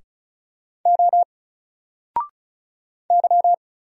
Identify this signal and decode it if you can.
Morse code, heard as a keyed tone, about 36 wpm, 700 Hz. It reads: O Y
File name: Single-Letter-Number-35wpm